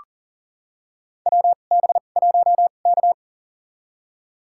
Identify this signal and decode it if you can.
Morse code, heard as a keyed tone, about 40 wpm, spaced as for 29 wpm, 700 Hz. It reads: WB1K
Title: USA Call Signs - 2x Character Spacing - 40wpm